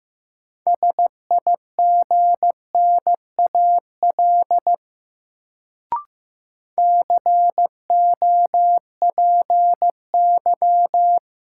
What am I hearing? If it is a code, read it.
Morse code, heard as a keyed tone, about 15 wpm, 700 Hz. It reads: SIGNAL COPY
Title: Sentences from Top 1400 Words - New Word Drill - 15wpm